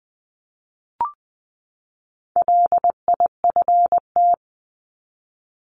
Morse code, heard as a keyed tone, about 20 wpm, 700 Hz. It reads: LIFT
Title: 260 Lesson 42e - Words 20-20wpm